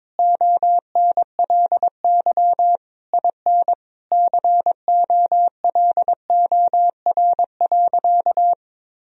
Morse code, heard as a keyed tone, about 22 wpm, 700 Hz. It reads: ONLY IN COLOR.
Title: Sentences from Top 1000 Words - Review Part 2 - 22wpm